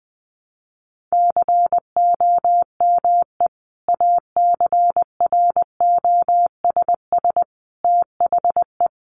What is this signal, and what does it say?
Morse code, heard as a keyed tone, about 19 wpm, 700 Hz. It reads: COME ACROSS THE
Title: Sentences from Top 700 Words 20wpm